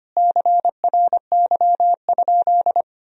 Morse code, heard as a keyed tone, about 25 wpm, 700 Hz. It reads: CRY?
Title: Sentences from Top 1000 Words - Review Part 2 - 25wpm